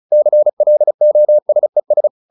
Morse code, heard as a keyed tone, about 34 wpm, 600 Hz. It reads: CLOSES